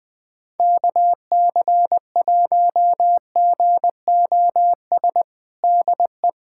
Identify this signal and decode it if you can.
Morse code, heard as a keyed tone, about 20 wpm, 700 Hz. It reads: KC1GOS DE
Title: USA Callsign DE Callsign - 6 Characters - 20wpm